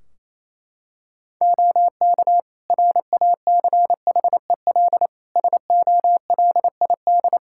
Morse code, heard as a keyed tone, about 27 wpm, 700 Hz. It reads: OK RACHEL SOLID
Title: K1USN SST QSO Practice 28wpm